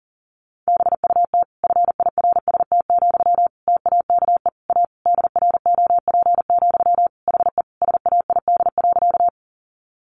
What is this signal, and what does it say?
Morse code, heard as a keyed tone, about 40 wpm, 700 Hz. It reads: BUT FIRST, TAKE A DROP, HE SAID.
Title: Sentences from Top 1400 Words 40wpm